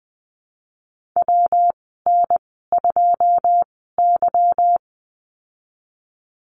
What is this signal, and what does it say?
Morse code, heard as a keyed tone, about 20 wpm, spaced as for 14 wpm, 700 Hz. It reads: WN2Y
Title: 161 Lesson 26g - Letter g - Call Signs 20wpm